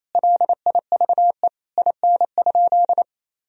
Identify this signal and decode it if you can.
Morse code, heard as a keyed tone, about 28 wpm, 700 Hz. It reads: LIVE IN?
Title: Sentences from Top 1000 Words - Review Part 2 - 28wpm